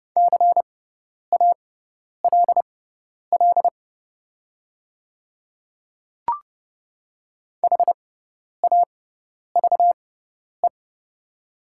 Morse code, heard as a keyed tone, about 30 wpm, spaced as for 10 wpm, 700 Hz. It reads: CALL HAVE
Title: 6x Character Spacing - Top 500 Words - 30wpm